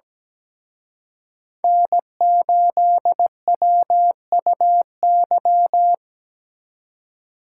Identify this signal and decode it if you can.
Morse code, heard as a keyed tone, about 20 wpm, 700 Hz. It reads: N8WUY
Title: Speed Racing - USA Call Signs 17wpm